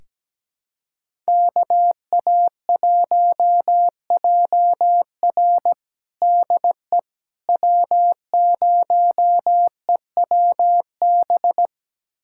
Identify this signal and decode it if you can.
Morse code, heard as a keyed tone, about 17 wpm, 700 Hz. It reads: KA1JR DE W0EWB
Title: USA Callsign DE Callsign - 5 Characters - 17wpm